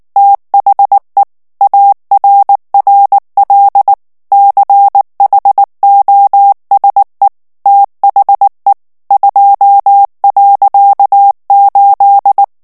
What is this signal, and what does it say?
Morse code, heard as a keyed tone, about 19 wpm, 800 Hz. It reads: THE ARRL CHOSE THE 2.8
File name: ANGLAIS texte clair 3 vitesse 18 mots minute